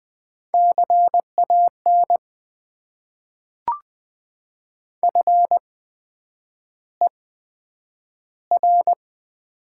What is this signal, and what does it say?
Morse code, heard as a keyed tone, about 20 wpm, spaced as for 9 wpm, 700 Hz. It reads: CAN FER
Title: Top 90 QSO Elements - 8x Character Spacing - 20wpm